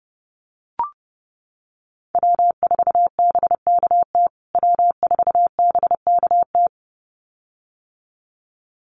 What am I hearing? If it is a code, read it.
Morse code, heard as a keyed tone, about 30 wpm, 700 Hz. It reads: W4BKT W4BKT
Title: USA Call Signs Repeated 2x 30wpm